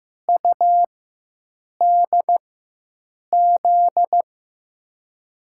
Morse code, heard as a keyed tone, about 15 wpm, spaced as for 7 wpm, 700 Hz. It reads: UDZ
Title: USA Call Signs - 4x Character Spacing - 15wpm